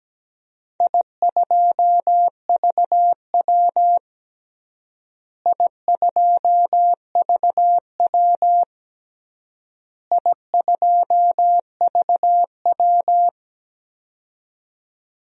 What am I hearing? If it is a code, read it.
Morse code, heard as a keyed tone, about 17 wpm, 700 Hz. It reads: I2VW I2VW I2VW
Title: DXCC-100 Call Signs - 4 Characters - Repeated 3x - 17wpm